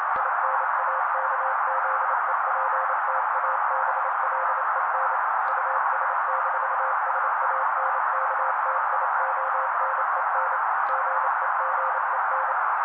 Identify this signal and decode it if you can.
Morse code, heard as a keyed tone, about 28 wpm, 530 Hz. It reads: INAKZEPTABLEN US-SANKTIONEN GEGEN
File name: daily / 27wpm noise